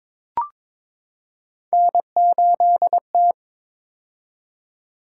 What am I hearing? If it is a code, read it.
Morse code, heard as a keyed tone, about 22 wpm, 700 Hz. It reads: N8T
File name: practice / 22wpm / Three Letter-Number Combinations - 22wpm